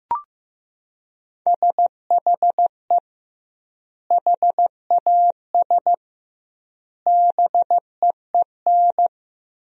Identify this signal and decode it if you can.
Morse code, heard as a keyed tone, about 15 wpm, 700 Hz. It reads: SHE HAS BEEN
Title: 2x Word-Spacing - Sentences from Top 200 Words - 15wpm